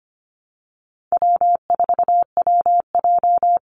Morse code, heard as a keyed tone, about 25 wpm, 700 Hz. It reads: W4WJ
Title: Easy Call Signs Repeated 2x - 25wpm